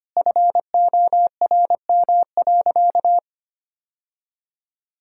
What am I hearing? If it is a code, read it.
Morse code, heard as a keyed tone, about 25 wpm, 700 Hz. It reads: FORM.
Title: Sentences from Top 1000 Words - Review Part 1 - 25wpm